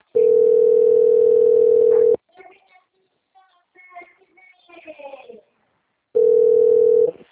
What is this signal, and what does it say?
Ringback tone (North America), 440 + 480 Hz.
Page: Rio Claro 2 - SP